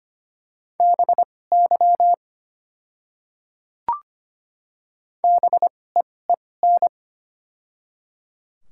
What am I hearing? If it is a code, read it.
Morse code, heard as a keyed tone, about 25 wpm, spaced as for 18 wpm, 700 Hz. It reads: BY BEEN
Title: 2x Character Spacing - Top 500 Words - 25wpm